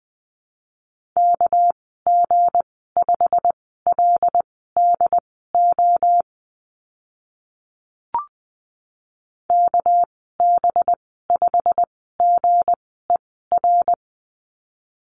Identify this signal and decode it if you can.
Morse code, heard as a keyed tone, about 20 wpm, spaced as for 15 wpm, 700 Hz. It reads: KG5LDO KB5GER
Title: 200 Lesson 32g - Letter k - Call Signs 20wpm